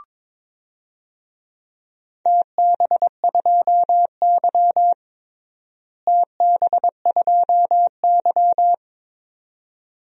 Morse code, heard as a keyed tone, about 22 wpm, 700 Hz. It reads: TB2Y TB2Y
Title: DXCC-100 Call Signs - 4 Characters - Repeated 2x - 22wpm